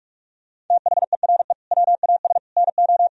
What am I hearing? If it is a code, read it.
Morse code, heard as a keyed tone, about 46 wpm, 700 Hz. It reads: THERE WAS NO
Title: Sentences from Top 1000 Words - Review Part 4 - 45wpm